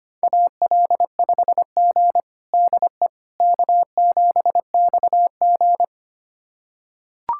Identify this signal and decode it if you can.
Morse code, heard as a keyed tone, about 25 wpm, 700 Hz. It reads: AL5G DE K7XG
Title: USA Callsign DE Callsign - 4 Characters - 25wpm